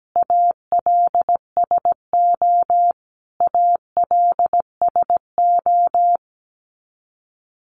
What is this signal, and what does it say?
Morse code, heard as a keyed tone, about 17 wpm, 700 Hz. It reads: ALSO ALSO
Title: Top 300 Words Repeated 17wpm